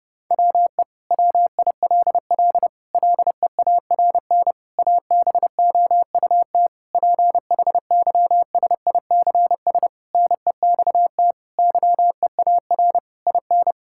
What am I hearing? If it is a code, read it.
Morse code, heard as a keyed tone, about 32 wpm, 700 Hz. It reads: WE WILL LEARN ABOUT PHYSICS NEXT YEAR IN
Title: Sentences from Top 1000 Words - Review Part 5 - 30wpm